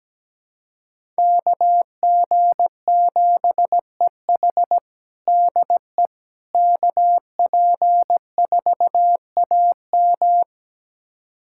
Morse code, heard as a keyed tone, about 17 wpm, 700 Hz. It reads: KG7EH DE KP4AM
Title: USA Callsign DE Callsign - 5 Characters - 17wpm